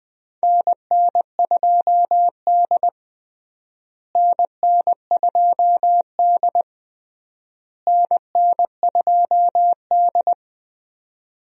Morse code, heard as a keyed tone, about 20 wpm, 700 Hz. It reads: NN2D NN2D NN2D
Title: USA Call Signs - 4 Characters - Repeated 3x - 20wpm